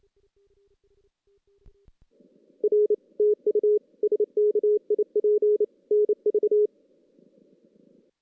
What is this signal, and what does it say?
Morse code, heard as a keyed tone, about 30 wpm, 440 Hz. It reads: R TU SKIP NV
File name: 2016 JAN CW NAQP / 2016 Jan naqp k5ot - 0-5